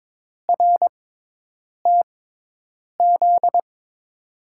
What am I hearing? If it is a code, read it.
Morse code, heard as a keyed tone, about 22 wpm, spaced as for 8 wpm, 700 Hz. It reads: RTZ
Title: USA Call Signs - 5 Characters - 6x Character Spacing - 22wpm